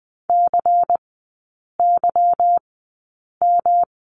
Morse code, heard as a keyed tone, about 20 wpm, spaced as for 8 wpm, 690 Hz. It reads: CYM
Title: 145 Lesson 24e - Letter y - Three Characters 20wpm